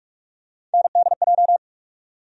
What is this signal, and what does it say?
Morse code, heard as a keyed tone, about 46 wpm, 700 Hz. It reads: NDJ
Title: USA Call Sign Postfixes - 3 Letters 45wpm